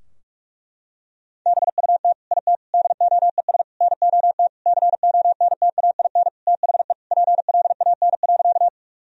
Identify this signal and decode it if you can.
Morse code, heard as a keyed tone, about 46 wpm, 700 Hz. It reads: BUT IT DOES NOT CONTAIN THE PLAN.
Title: Sentences from Top 1000 Words - Review Part 2 - 45wpm